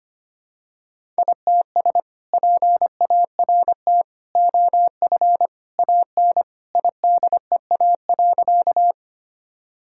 Morse code, heard as a keyed tone, about 25 wpm, 700 Hz. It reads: ITS PART OF AN IDEA.
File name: practice / 25wpm / Sentences with 2-Word Phrases - 25wpm